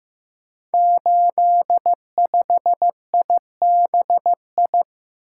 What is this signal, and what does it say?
Morse code, heard as a keyed tone, about 15 wpm, 700 Hz. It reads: 85IBI
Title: Code Groups - 5 Letter:Numbers - 15wpm